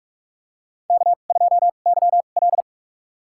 Morse code, heard as a keyed tone, about 46 wpm, spaced as for 32 wpm, 700 Hz. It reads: K2YL
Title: USA Call Signs - 2x Character Spacing - 45wpm